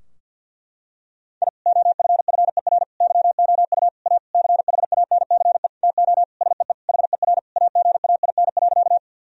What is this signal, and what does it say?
Morse code, heard as a keyed tone, about 52 wpm, 700 Hz. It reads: I OFFER YOU A CHANCE TO SEE HER AGAIN.
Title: Sentences from Top 1000 Words - Review Part 4 - 50wpm